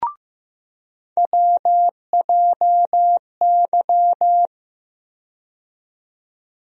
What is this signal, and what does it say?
Morse code, heard as a keyed tone, about 15 wpm, 700 Hz. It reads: WJY
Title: USA Call Sign Postfixes - 3 Letters 15wpm